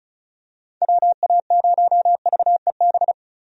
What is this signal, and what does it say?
Morse code, heard as a keyed tone, about 36 wpm, 700 Hz. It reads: WA0VEB
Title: USA Call Signs - 6 Characters - 35wpm